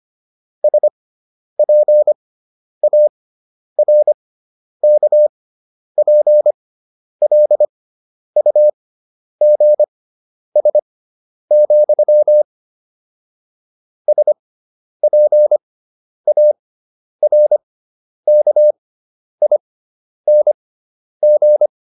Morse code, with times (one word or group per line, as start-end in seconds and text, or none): 0.6-12.4 SPARKPLUGS,
14.1-21.7 SPARKING